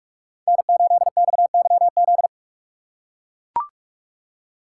Morse code, heard as a keyed tone, about 43 wpm, 700 Hz. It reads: N8XYZ